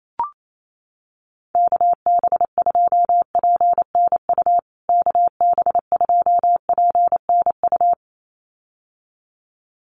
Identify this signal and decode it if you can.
Morse code, heard as a keyed tone, about 28 wpm, 700 Hz. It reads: KB2PNU KB2PNU
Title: USA Call Signs Repeated 2x 28wpm